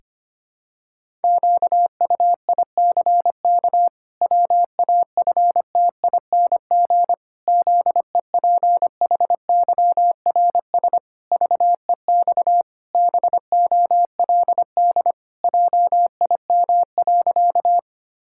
Morse code, heard as a keyed tone, about 25 wpm, 700 Hz. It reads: QUICK WAFTING ZEPHYRS VEX BOLD JIM.
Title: Pangram Challenge v2 - 25wpm